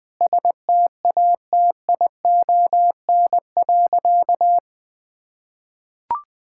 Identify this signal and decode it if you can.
Morse code, heard as a keyed tone, about 20 wpm, 700 Hz. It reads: STATION.
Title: Difficult Sentences with Hints - 20wpm